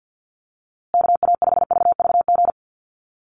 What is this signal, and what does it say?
Morse code, heard as a keyed tone, about 48 wpm, 700 Hz. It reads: KA5VVG
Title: USA Call Signs Repeated 3x 50wpm